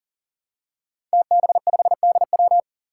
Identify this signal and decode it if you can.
Morse code, heard as a keyed tone, about 38 wpm, 700 Hz. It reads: TB5DW
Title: Five Challenging Letter-Number Combinations 40wpm